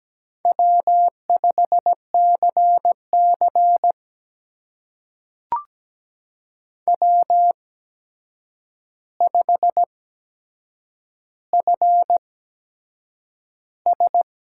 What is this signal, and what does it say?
Morse code, heard as a keyed tone, about 17 wpm, spaced as for 7 wpm, 700 Hz. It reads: W5CC W5FS
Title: USA Call Signs - 8x Character Spacing - 17wpm